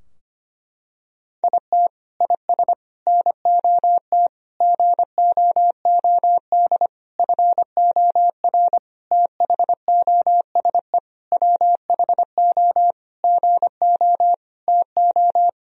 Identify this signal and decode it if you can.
Morse code, heard as a keyed tone, about 25 wpm, 700 Hz. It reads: IT IS NOT GOOD FOR THOSE WHO GO TO
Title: Sentences from Top 1000 Words - Review Part 2 - 25wpm